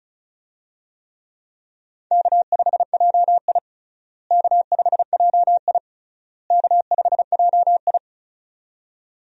Morse code, heard as a keyed tone, about 36 wpm, 700 Hz. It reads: K5JI K5JI K5JI
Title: USA Call Signs - 4 Characters - Repeated 3x - 35wpm